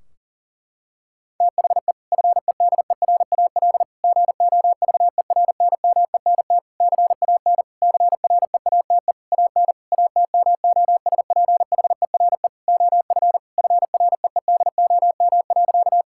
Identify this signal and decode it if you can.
Morse code, heard as a keyed tone, about 40 wpm, 700 Hz. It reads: THE FEDERAL GOVERNMENT CAN CREATE AN ATMOSPHERE OF FREEDOM.
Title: Sentences from Top 1600 Words 40wpm